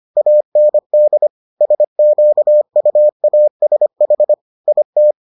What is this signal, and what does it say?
Morse code, heard as a keyed tone, about 25 wpm, 600 Hz. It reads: AND SQUASH IT